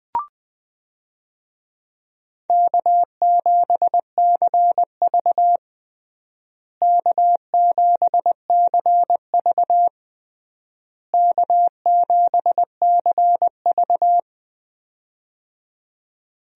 Morse code, heard as a keyed tone, about 20 wpm, 700 Hz. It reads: K7CV K7CV K7CV
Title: USA Call Signs - 4 Characters - Repeated 3x - 20wpm